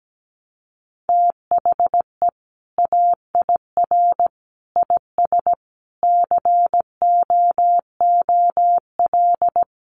Morse code, heard as a keyed tone, about 17 wpm, 700 Hz. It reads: THE AIR IS COOL
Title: Sentences from Top 1200 Words 17wpm